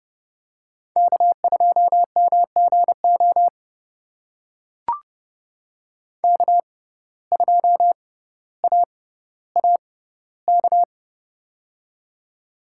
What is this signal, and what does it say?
Morse code, heard as a keyed tone, about 30 wpm, spaced as for 15 wpm, 700 Hz. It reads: K2MGO K2AAK
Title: USA Call Signs - 5 Characters - 6x Character Spacing - 30wpm